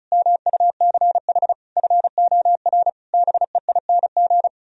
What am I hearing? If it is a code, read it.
Morse code, heard as a keyed tone, about 45 wpm, 700 Hz. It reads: MUCH FOR BEING